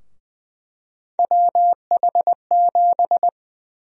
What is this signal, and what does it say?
Morse code, heard as a keyed tone, about 20 wpm, 700 Hz. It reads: WH7
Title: USA Call Sign Prefixes - 20wpm